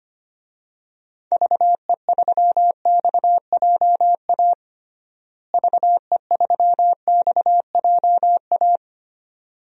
Morse code, heard as a keyed tone, about 25 wpm, 700 Hz. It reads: VE3XJA VE3XJA
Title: DXCC-100 Call Signs - 6 Characters - Repeated 2x - 25wpm